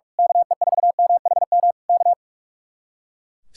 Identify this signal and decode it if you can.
Morse code, heard as a keyed tone, about 41 wpm, 700 Hz. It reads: KE4MHM K
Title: Speed-Racing Common QSO Phrases - 45wpm